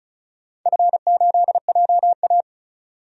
Morse code, heard as a keyed tone, about 34 wpm, 700 Hz. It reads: F8JA
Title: DXCC-100 Call Signs - 4 Characters - 35wpm